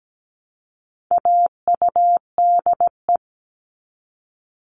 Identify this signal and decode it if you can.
Morse code, heard as a keyed tone, about 17 wpm, 700 Hz. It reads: AUDE
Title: All Cut Numbers - 4-Digit 17wpm